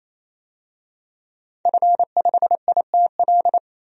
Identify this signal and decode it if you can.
Morse code, heard as a keyed tone, about 27 wpm, 700 Hz. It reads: F5ITL
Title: DXCC-100 Call Signs - 5 Characters - 28wpm